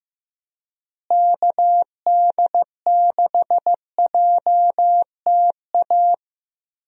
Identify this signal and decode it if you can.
Morse code, heard as a keyed tone, about 16 wpm, 700 Hz. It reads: KD6JTA
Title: USA Call Signs - 6 Characters - Repeated 3x - 15wpm